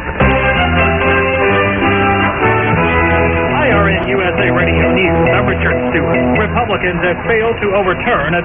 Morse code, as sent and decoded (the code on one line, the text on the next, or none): none